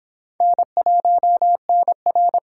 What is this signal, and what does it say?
Morse code, heard as a keyed tone, about 25 wpm, 700 Hz. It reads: N1NR